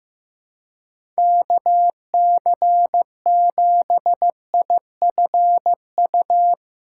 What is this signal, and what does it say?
Morse code, heard as a keyed tone, about 15 wpm, 700 Hz. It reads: KC7IFU
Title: USA Call Signs - 6 Characters - 8x Character Spacing - 15wpm